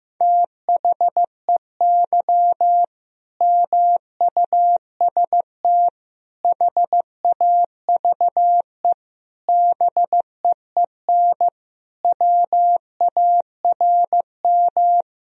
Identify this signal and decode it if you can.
Morse code, heard as a keyed tone, about 15 wpm, 700 Hz. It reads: THEY MUST HAVE BEEN WARM
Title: Sentences from Top 1000 Words - Review Part 2 - 15wpm